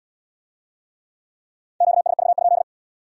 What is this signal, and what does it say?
Morse code, heard as a keyed tone, about 75 wpm, 700 Hz. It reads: QSV?